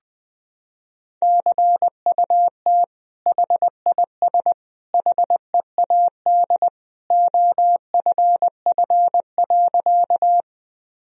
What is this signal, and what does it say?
Morse code, heard as a keyed tone, about 20 wpm, 700 Hz. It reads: CUT HIS HEAD OFF.
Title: Sentences from Top 1000 Words - Review Part 1 - 20wpm